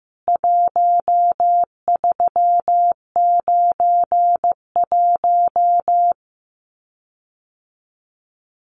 Morse code, heard as a keyed tone, about 15 wpm, 700 Hz. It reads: 1391
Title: CWT Exchange 15wpm